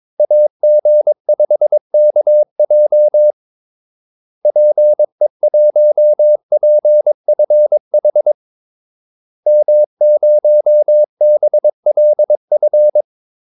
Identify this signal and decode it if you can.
Morse code, heard as a keyed tone, about 22 wpm, 600 Hz. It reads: AG5KJ PE1PFH M0BLF
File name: Callsigns-22wpm